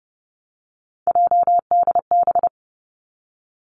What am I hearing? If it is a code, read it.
Morse code, heard as a keyed tone, about 32 wpm, 700 Hz. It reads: JDB
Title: Three Challenging Letter-Number Combinations 30wpm